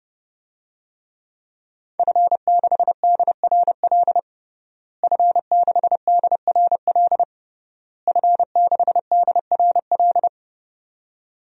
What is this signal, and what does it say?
Morse code, heard as a keyed tone, about 30 wpm, 700 Hz. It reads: F6DRL F6DRL F6DRL
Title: DXCC-100 Call Signs - 5 Characters - Repeated 3x - 30wpm